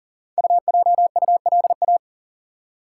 Morse code, heard as a keyed tone, about 39 wpm, 690 Hz. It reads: UJULA
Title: Code Groups - 5 Letters - 40wpm